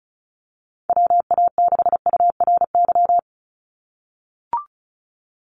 Morse code, heard as a keyed tone, about 34 wpm, 700 Hz. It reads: WA6URY
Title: Call Signs - 35wpm